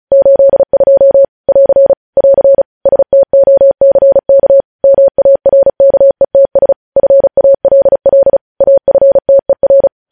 Morse code, heard as a keyed tone, about 36 wpm, 550 Hz. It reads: 82 <AR> <AR> STOCK MARKETS FALL AFTER
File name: podcast / News headlines-1x-35x35